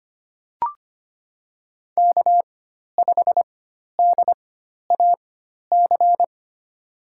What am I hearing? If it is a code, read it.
Morse code, heard as a keyed tone, about 25 wpm, spaced as for 12 wpm, 700 Hz. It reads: K5DAC
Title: USA Call Signs - 5 Characters - 4x Character Spacing - 25wpm